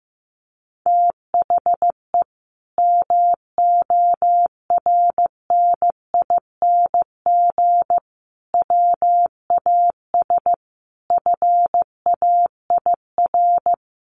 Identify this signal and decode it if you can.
Morse code, heard as a keyed tone, about 14 wpm, 700 Hz. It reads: THE MORNING WAS FAIR
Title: Sentences from Top 1300 Words 15wpm